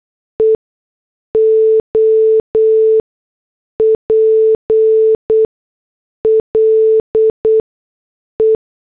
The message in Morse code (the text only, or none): EOPLE